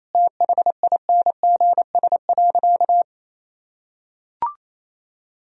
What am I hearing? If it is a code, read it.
Morse code, heard as a keyed tone, about 28 wpm, 700 Hz. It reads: THINGS.
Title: Sentences from Top 1000 Words - Review Part 4 - 28wpm